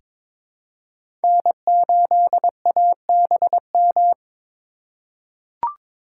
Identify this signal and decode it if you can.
Morse code, heard as a keyed tone, about 22 wpm, 700 Hz. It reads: N8ABM
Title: USA Call Signs - 5 Characters - 4x Character Spacing - 22wpm